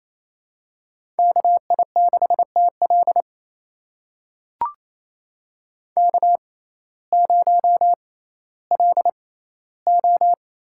Morse code, heard as a keyed tone, about 28 wpm, spaced as for 15 wpm, 700 Hz. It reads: KI6TL K0LO
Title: USA Call Signs - 6x Character Spacing - 28wpm